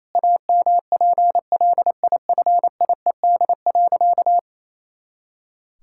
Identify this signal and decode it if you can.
Morse code, heard as a keyed tone, about 28 wpm, 700 Hz. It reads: AMPLIFIED.
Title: Pangram Challenge v2 - 28wpm